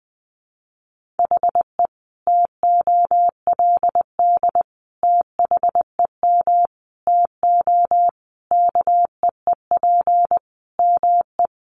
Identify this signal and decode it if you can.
Morse code, heard as a keyed tone, about 20 wpm, 700 Hz. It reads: HE TOLD THEM TO KEEP ME
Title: Sentences from Top 1200 Words 20wpm